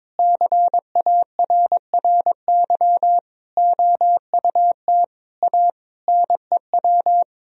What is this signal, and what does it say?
Morse code, heard as a keyed tone, about 22 wpm, 700 Hz. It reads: CARRY OUT A NEW
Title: Sentences from Top 1000 Words - Review Part 1 - 22wpm